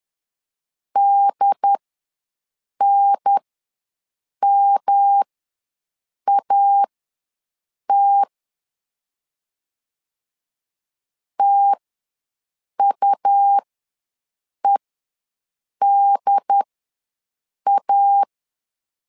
Morse code, heard as a keyed tone, about 10 wpm, spaced as for 6 wpm, 800 Hz. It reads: DNMAT TUEDA
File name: LEZIONI / 3 MWRO / 3 M ws 10wpm 2121